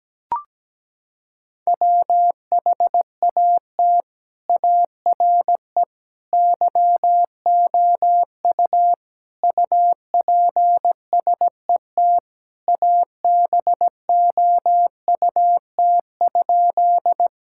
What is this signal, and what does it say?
Morse code, heard as a keyed tone, about 17 wpm, 700 Hz. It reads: WHAT ARE YOU UPSET ABOUT?
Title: Sentences from Top 1000 Words - Review Part 5 - 17wpm